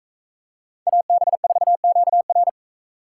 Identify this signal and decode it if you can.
Morse code, heard as a keyed tone, about 43 wpm, 700 Hz. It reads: AB4QR